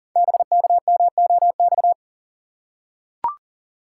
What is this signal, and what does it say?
Morse code, heard as a keyed tone, about 40 wpm, 700 Hz. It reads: BKMOX
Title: Code Groups - 5 Letters - 40wpm